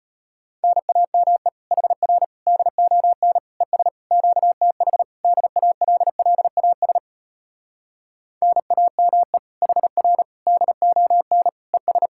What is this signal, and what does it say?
Morse code, heard as a keyed tone, about 33 wpm, 700 Hz. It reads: NAME HR DON ES QTH DALLAS NAME HR DON ES